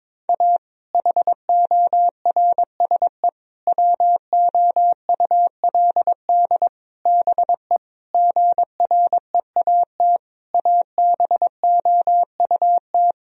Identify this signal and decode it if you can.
Morse code, heard as a keyed tone, about 22 wpm, 700 Hz. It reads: A HORSE WOULD BE GREAT ABOUT
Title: Sentences from Top 1000 Words - Review Part 1 - 22wpm